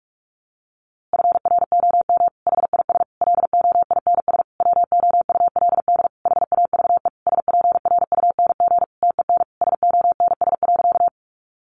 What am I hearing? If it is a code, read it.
Morse code, heard as a keyed tone, about 44 wpm, 700 Hz. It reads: FROM HIS LOINS WOULD HAVE SPRUNG TEN SONS.
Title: A Sound Of Thunder 45wpm